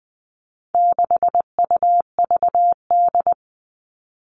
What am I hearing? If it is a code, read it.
Morse code, heard as a keyed tone, about 20 wpm, 700 Hz. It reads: 6UVD
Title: All Cut Numbers - 4-Digit 20wpm